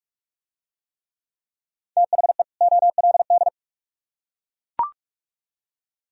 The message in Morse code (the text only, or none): THE OLD